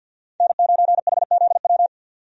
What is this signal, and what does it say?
Morse code, heard as a keyed tone, about 50 wpm, 700 Hz. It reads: N9HZW
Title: USA Call Signs - 5 Characters - 50wpm